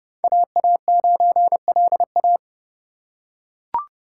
Morse code, heard as a keyed tone, about 29 wpm, 700 Hz. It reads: AA9LA